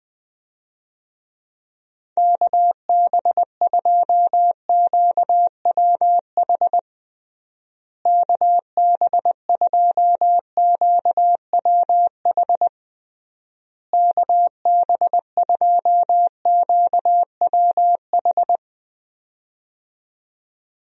Morse code, heard as a keyed tone, about 20 wpm, 700 Hz. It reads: KB2QWH KB2QWH KB2QWH
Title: USA Call Signs - 6 Characters - Repeated 3x - 20wpm